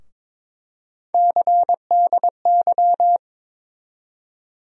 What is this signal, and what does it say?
Morse code, heard as a keyed tone, about 22 wpm, 700 Hz. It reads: CDY
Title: USA Call Sign Postfixes - 3 Letters 22wpm